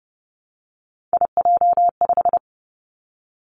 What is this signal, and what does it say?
Morse code, heard as a keyed tone, about 30 wpm, 700 Hz. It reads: IJ5
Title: Three Challenging Letter-Number Combinations 30wpm